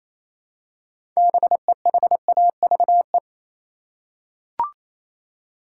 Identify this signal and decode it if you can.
Morse code, heard as a keyed tone, about 28 wpm, 700 Hz. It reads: BEHAVE
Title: Six Letter Words - 28wpm